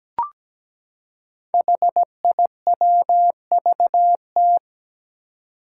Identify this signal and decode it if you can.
Morse code, heard as a keyed tone, about 17 wpm, 700 Hz. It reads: HIWVT
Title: Five Challenging Letter-Number Combinations 17wpm